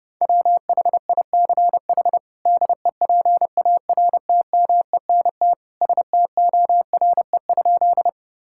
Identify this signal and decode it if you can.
Morse code, heard as a keyed tone, about 30 wpm, 700 Hz. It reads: WHICH DEPARTMENT STORE?
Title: Sentences from Top 1000 Words - Review Part 5 - 30wpm